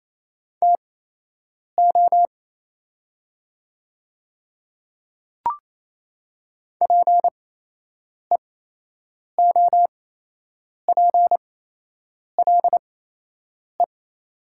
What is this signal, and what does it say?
Morse code, heard as a keyed tone, about 28 wpm, spaced as for 8 wpm, 700 Hz. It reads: TO PEOPLE
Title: 8x Character Spacing - Top 100 Words - 28wpm